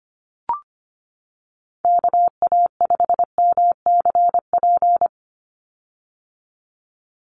Morse code, heard as a keyed tone, about 25 wpm, 700 Hz. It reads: KA5MCP
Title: USA Call Signs 25wpm